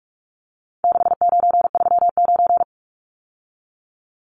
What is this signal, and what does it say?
Morse code, heard as a keyed tone, about 44 wpm, 700 Hz. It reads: 6939
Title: Numbers - 4-Digit 45wpm